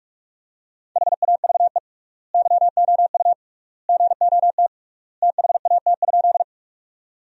Morse code, heard as a keyed tone, about 42 wpm, 700 Hz. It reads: HAVE YOU GOT THAT?
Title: 3x Word-Spacing - Sentences from Top 300 Words - 45wpm